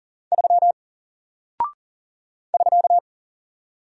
Morse code, heard as a keyed tone, about 39 wpm, 700 Hz. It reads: 3 <SK>
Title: Single Character (Letter:Number:Punctuation:Prosign) 40wpm